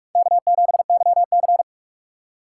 Morse code, heard as a keyed tone, about 46 wpm, 700 Hz. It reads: K7YC
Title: USA Call Signs - 4 Characters - 6x Character Spacing - 45wpm